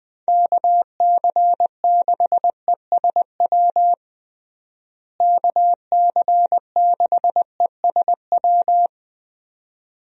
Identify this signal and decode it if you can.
Morse code, heard as a keyed tone, about 20 wpm, 700 Hz. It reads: KC6ESW KC6ESW
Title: USA Call Signs - 6 Characters - Repeated 2x - 20wpm